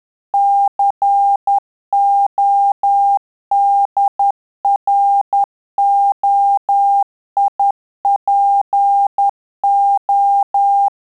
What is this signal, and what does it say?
Morse code, heard as a keyed tone, about 10 wpm, 800 Hz. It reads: CODROIPO
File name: Telegraphy-group / QSO CW 10 PPM